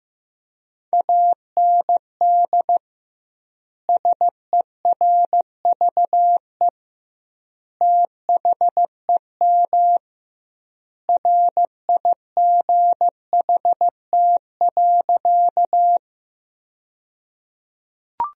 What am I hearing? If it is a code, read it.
Morse code, heard as a keyed tone, about 16 wpm, 700 Hz. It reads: AND SERVE THEM RIGHT.
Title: 2x Word-Spacing - Sentences from Top 400 Words - 15wpm